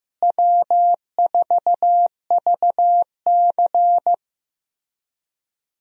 Morse code, heard as a keyed tone, about 15 wpm, 700 Hz. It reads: W4VC
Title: USA Call Signs - 4 Characters - Repeated 3x - 15wpm